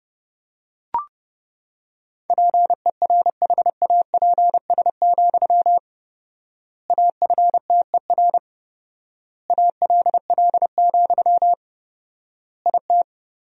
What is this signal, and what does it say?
Morse code, heard as a keyed tone, about 30 wpm, 700 Hz. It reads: PERHAPS, AFTER ALL, IT
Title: 4x Word-Spacing - Sentences from Top 500 Words - 30wpm